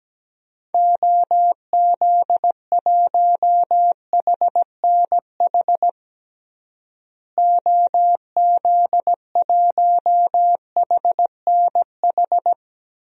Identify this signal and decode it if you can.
Morse code, heard as a keyed tone, about 17 wpm, 700 Hz. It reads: OZ1HNH OZ1HNH
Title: DXCC-100 Call Signs - 6 Characters - Repeated 3x - 17wpm